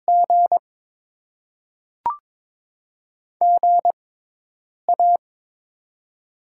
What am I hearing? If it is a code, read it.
Morse code, heard as a keyed tone, about 22 wpm, 700 Hz. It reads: G G A
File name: Top 20 QSO Elements - 6x Character Spacing - 22wpm